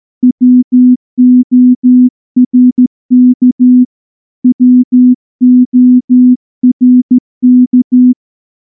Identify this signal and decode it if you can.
Morse code, heard as a keyed tone, about 14 wpm, 250 Hz. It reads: WORK WORK
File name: morse-training-15WPM